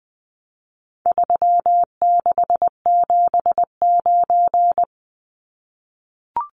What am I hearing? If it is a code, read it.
Morse code, heard as a keyed tone, about 20 wpm, 700 Hz. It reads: 3679
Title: Numbers - 4-Digit 20wpm